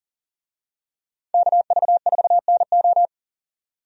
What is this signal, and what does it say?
Morse code, heard as a keyed tone, about 40 wpm, 700 Hz. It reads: KV4NO
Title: USA Call Signs - 5 Characters - 40wpm